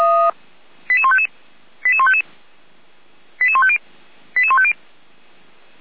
Five-tone selective call (ZVEI), heard as quick runs of five tones, tones of about 70 ms. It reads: 88155 88155 88150 88150